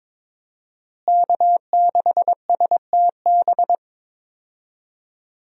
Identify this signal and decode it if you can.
Morse code, heard as a keyed tone, about 22 wpm, 700 Hz. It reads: K6STB
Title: USA Call Signs - 5 Characters - Repeated 3x - 22wpm